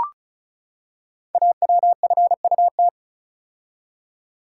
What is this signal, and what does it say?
Morse code, heard as a keyed tone, about 36 wpm, 700 Hz. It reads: AWFUT
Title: Code Groups - 5 Letters - 35wpm